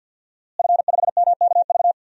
Morse code, heard as a keyed tone, about 51 wpm, 700 Hz. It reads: F5MKV
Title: DXCC-100 Call Signs - 5 Characters - Repeated 3x - 50wpm